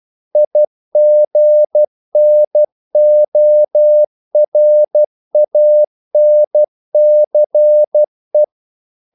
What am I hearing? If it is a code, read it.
Morse code, heard as a keyed tone, about 12 wpm, 600 Hz. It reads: IGNORANCE
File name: CW / Practice / EnglishQuotations0112